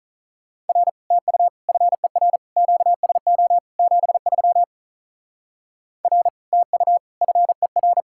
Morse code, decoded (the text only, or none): R TU FER QSO 73 R TU FER